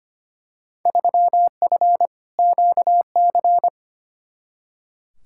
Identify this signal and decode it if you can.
Morse code, heard as a keyed tone, about 25 wpm, 700 Hz. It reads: 3F QC
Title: Field Day Exchanges - 25wpm